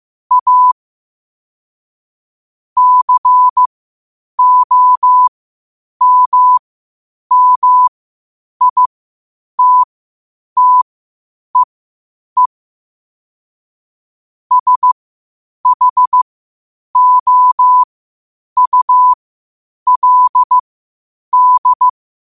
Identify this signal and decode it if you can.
Morse code, heard as a keyed tone, about 15 wpm, spaced as for 8 wpm, 1000 Hz. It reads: A COMMITTEE SHOULD
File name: qotdfiles / 8wpm